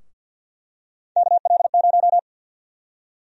Morse code, heard as a keyed tone, about 52 wpm, 700 Hz. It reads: KL0
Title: USA Call Sign Prefixes - 50wpm